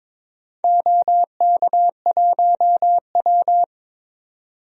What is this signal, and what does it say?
Morse code, heard as a keyed tone, about 22 wpm, 700 Hz. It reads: OK1W
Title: DXCC-100 Call Signs - 4 Characters - 22wpm